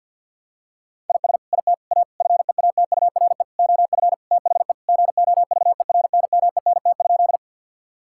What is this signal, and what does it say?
Morse code, heard as a keyed tone, about 50 wpm, 700 Hz. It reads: IS IT A FEATURE OF THE GOVERNMENT?
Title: Sentences from Top 1000 Words - Review Part 5 - 50wpm